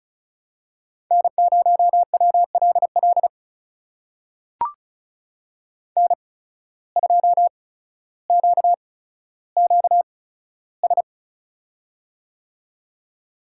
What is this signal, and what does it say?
Morse code, heard as a keyed tone, about 36 wpm, spaced as for 15 wpm, 700 Hz. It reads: N0WLL N2QQS
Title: USA Call Signs - 5 Characters - 8x Character Spacing - 35wpm